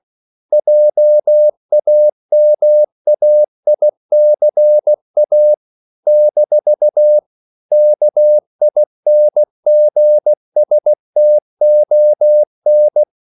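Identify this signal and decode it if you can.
Morse code, heard as a keyed tone, about 16 wpm, 600 Hz. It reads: JAMAICA - KINGSTON
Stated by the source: CW practice of national capitals at 16 words per minute, from ON6ZQ.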